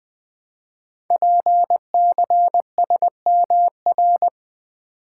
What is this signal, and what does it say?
Morse code, heard as a keyed tone, about 20 wpm, 700 Hz. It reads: PCSMR
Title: Code Groups - 5 Letter:Numbers - 20wpm